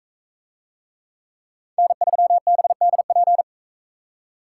Morse code, heard as a keyed tone, about 40 wpm, 700 Hz. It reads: N3BDP